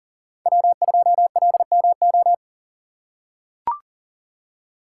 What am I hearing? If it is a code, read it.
Morse code, heard as a keyed tone, about 42 wpm, 690 Hz. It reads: W2LMO
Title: USA Call Signs - 5 Characters - 6x Character Spacing - 40wpm